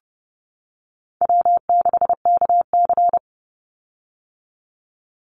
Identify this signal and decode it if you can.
Morse code, heard as a keyed tone, about 30 wpm, 700 Hz. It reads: W6KC
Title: Call Signs - 30wpm